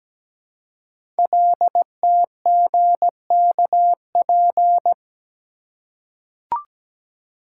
Morse code, heard as a keyed tone, about 17 wpm, 700 Hz. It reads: LTGKP
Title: Code Groups - 5 Letter:Numbers - 17wpm